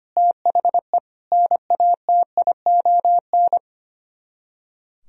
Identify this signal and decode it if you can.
Morse code, heard as a keyed tone, about 29 wpm, 700 Hz. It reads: THE NATION
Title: Speed-Racing - Sets of 3 Words 25wpm